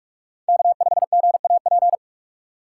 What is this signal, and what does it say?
Morse code, heard as a keyed tone, about 45 wpm, 700 Hz. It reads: K5GAP
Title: USA Call Signs - 5 Characters - 3x Character Spacing - 45wpm